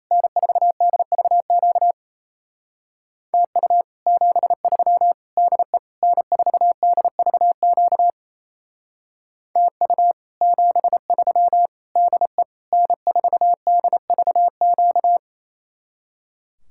Morse code, heard as a keyed tone, about 34 wpm, 700 Hz. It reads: N4DVQ TU 73 DE N4DVQ TU 73 DE N4DVQ
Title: Speed-Racing Common QSO Phrases Encore - 28wpm